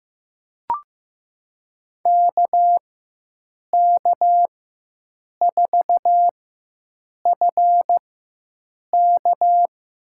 Morse code, heard as a keyed tone, about 15 wpm, spaced as for 7 wpm, 700 Hz. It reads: KK4FK
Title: USA Call Signs - 6 Characters - 4x Character Spacing - 15wpm